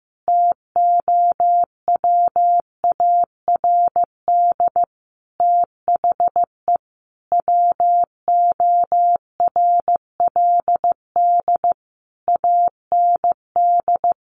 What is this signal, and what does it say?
Morse code, heard as a keyed tone, about 15 wpm, 700 Hz. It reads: TOWARD THE WORLD AND
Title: Sentences from Top 1300 Words 15wpm